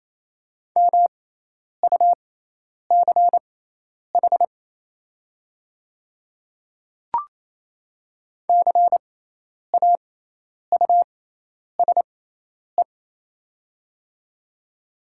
Morse code, heard as a keyed tone, about 28 wpm, spaced as for 10 wpm, 700 Hz. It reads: MUCH CAUSE
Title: 6x Character Spacing - Top 200 Words - 28wpm